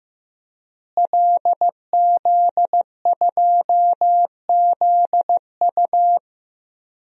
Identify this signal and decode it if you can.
Morse code, heard as a keyed tone, about 15 wpm, 700 Hz. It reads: LZ2ZU
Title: DXCC-100 Call Signs - 5 Characters - Repeated 2x - 15wpm